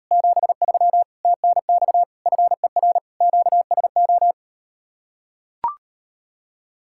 Morse code, heard as a keyed tone, about 38 wpm, 700 Hz. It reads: 73 TNX FER QSO